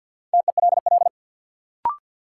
Morse code, heard as a keyed tone, about 52 wpm, 700 Hz. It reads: TELL
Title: Sets of 2 Words - New Word Drill - 50wpm